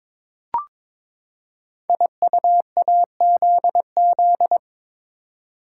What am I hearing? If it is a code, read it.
Morse code, heard as a keyed tone, about 22 wpm, 700 Hz. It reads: IUAZZ
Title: Code Groups - 5 Letters - 22wpm